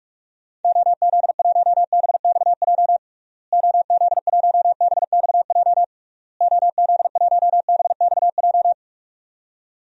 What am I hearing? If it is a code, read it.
Morse code, heard as a keyed tone, about 46 wpm, 700 Hz. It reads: OZ1BXJ OZ1BXJ OZ1BXJ
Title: DXCC-100 Call Signs - 6 Characters - Repeated 3x - 45wpm